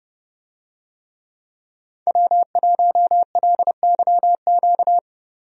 Morse code, heard as a keyed tone, about 30 wpm, 700 Hz. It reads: W1LYQ
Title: USA Call Signs - 5 Characters - 30wpm